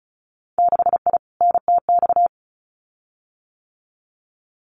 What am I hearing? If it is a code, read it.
Morse code, heard as a keyed tone, about 36 wpm, 700 Hz. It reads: 6I NTX
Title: Winter Field Day Exchanges Encore 35wpm